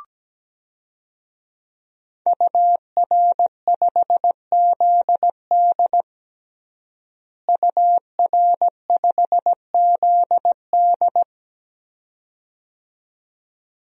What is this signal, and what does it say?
Morse code, heard as a keyed tone, about 17 wpm, 700 Hz. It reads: UR5ZD UR5ZD
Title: DXCC-100 Call Signs - 5 Characters - Repeated 2x - 17wpm